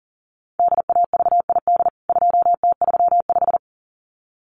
Morse code, heard as a keyed tone, about 40 wpm, 700 Hz. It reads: DAVID 2T35
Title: CWT Exchange 40wpm